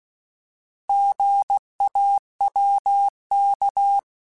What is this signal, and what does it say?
Morse code, heard as a keyed tone, about 16 wpm, 800 Hz. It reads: GAWK